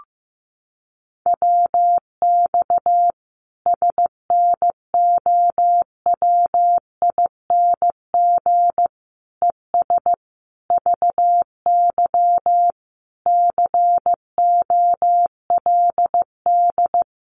Morse code, heard as a keyed tone, about 15 wpm, 700 Hz. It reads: WX SNOWING ES VY COLD
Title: Common QSO Phrases 15wpm